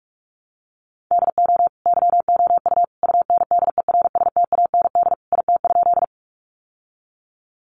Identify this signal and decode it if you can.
Morse code, heard as a keyed tone, about 44 wpm, 700 Hz. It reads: DO YOU UNDERSTAND IT?
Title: Sentences from Top 600 Words 45wpm